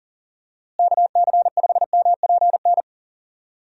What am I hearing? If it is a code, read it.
Morse code, heard as a keyed tone, about 40 wpm, 700 Hz. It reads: KC5MPN
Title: USA Call Signs - 6 Characters - Repeated 2x - 40wpm